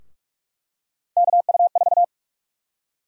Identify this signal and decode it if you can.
Morse code, heard as a keyed tone, about 46 wpm, 700 Hz. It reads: KU4
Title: USA Call Sign Prefixes - 45wpm